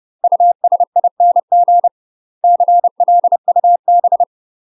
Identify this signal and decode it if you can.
Morse code, heard as a keyed tone, about 30 wpm, 700 Hz. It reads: USING CLUB
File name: UK MASTER QSOs TRAINING / 30WPM / UK MASTER QSOs 0016